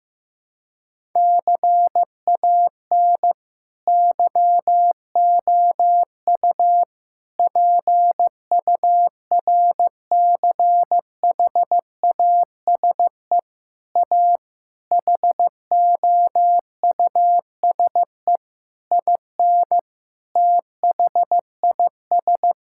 Morse code, heard as a keyed tone, about 15 wpm, 700 Hz. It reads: CAN YOU PURCHASE A HOUSE IN THIS
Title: Sentences from Top 1000 Words - Review Part 5 - 15wpm